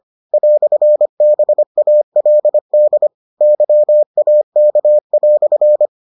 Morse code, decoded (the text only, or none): "BALD YAK"